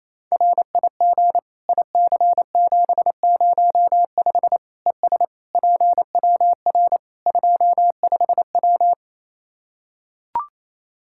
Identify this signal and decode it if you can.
Morse code, heard as a keyed tone, about 28 wpm, 700 Hz. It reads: RIG IC705 ES PWR 25W
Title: Practice QSO - Exchange 2 - Info - 28wpm